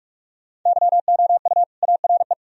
Morse code, heard as a keyed tone, about 45 wpm, 700 Hz. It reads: YOU ARE
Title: Sentences from Top 1000 Words - Review Part 4 - 45wpm